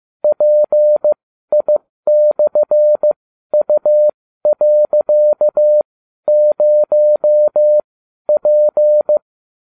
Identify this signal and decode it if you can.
Morse code, heard as a keyed tone, about 15 wpm, spaced as for 12 wpm, 600 Hz. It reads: PI/U.0P